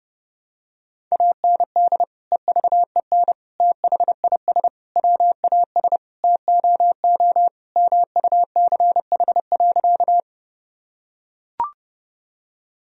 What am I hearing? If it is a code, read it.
Morse code, heard as a keyed tone, about 30 wpm, 700 Hz. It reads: AND EVEN THIS WAS TOO MUCH.
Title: Sentences from Top 1000 Words - Review Part 1 - 30wpm